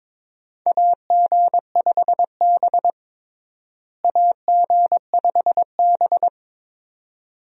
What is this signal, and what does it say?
Morse code, heard as a keyed tone, about 22 wpm, 700 Hz. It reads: AG5B AG5B
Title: USA Call Signs - 4 Characters - Repeated 2x - 22wpm